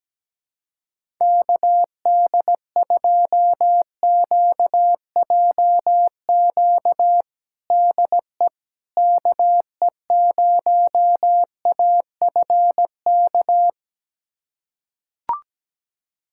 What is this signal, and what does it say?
Morse code, heard as a keyed tone, about 17 wpm, 700 Hz. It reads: KD2QJQ DE KE0AFK
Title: USA Callsign DE Callsign 17wpm